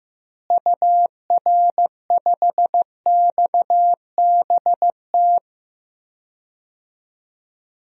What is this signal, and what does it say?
Morse code, heard as a keyed tone, about 15 wpm, 700 Hz. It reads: UR5XBT
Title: DXCC-100 Call Signs - 6 Characters - Repeated 3x - 15wpm